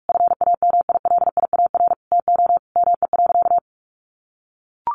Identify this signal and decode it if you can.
Morse code, heard as a keyed tone, about 44 wpm, 700 Hz. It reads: FAMILIAR TO ME.
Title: Sentences from Top 1100 Words 45wpm